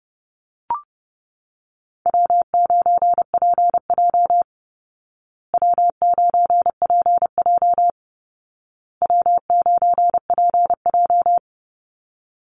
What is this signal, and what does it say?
Morse code, heard as a keyed tone, about 30 wpm, 700 Hz. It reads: W9PJ W9PJ W9PJ
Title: Easy Call Signs Repeated 3x - 30wpm